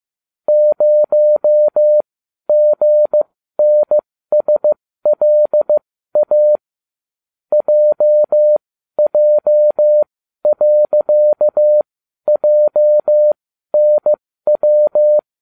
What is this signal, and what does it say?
Morse code, heard as a keyed tone, about 15 wpm, spaced as for 12 wpm, 600 Hz. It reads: 0GNSLA JJ.JNW